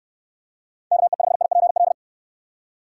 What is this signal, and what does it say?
Morse code, heard as a keyed tone, about 69 wpm, 700 Hz. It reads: KE5IPL